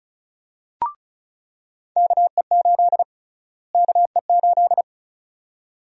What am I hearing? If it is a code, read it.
Morse code, heard as a keyed tone, about 34 wpm, 700 Hz. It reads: KE8 KE8
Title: USA Call Sign Prefixes - Repeated 2x - 35wpm